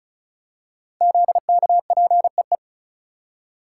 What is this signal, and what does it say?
Morse code, heard as a keyed tone, about 36 wpm, 700 Hz. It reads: ZKPEE
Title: Code Groups - 5 Letters - 35wpm